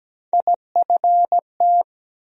Morse code, heard as a keyed tone, about 17 wpm, 700 Hz. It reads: IFT
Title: USA Call Sign Postfixes - 3 Letters 17wpm